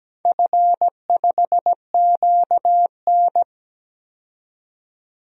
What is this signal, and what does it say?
Morse code, heard as a keyed tone, about 17 wpm, 700 Hz. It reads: F5QN
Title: DXCC-100 Call Signs - 4 Characters - 17wpm